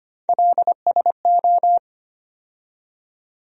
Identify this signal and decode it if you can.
Morse code, heard as a keyed tone, about 25 wpm, 700 Hz. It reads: LSO
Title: USA Call Sign Postfixes - 3 Letters 25wpm